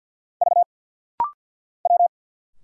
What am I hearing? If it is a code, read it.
Morse code, heard as a keyed tone, about 49 wpm, 700 Hz. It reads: V W
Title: Single Letter-Number Alt - Rapid-Fire 50wpm